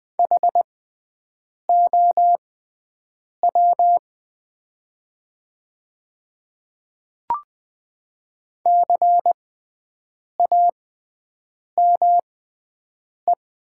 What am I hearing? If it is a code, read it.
Morse code, heard as a keyed tone, about 20 wpm, spaced as for 7 wpm, 700 Hz. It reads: HOW CAME
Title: 6x Character Spacing - Top 300 Words - 20wpm